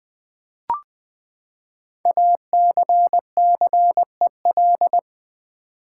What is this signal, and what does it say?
Morse code, heard as a keyed tone, about 20 wpm, 700 Hz. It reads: ACCEL
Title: Build A Word - 5 Letters At A Time - 10-Letter Words - 20wpm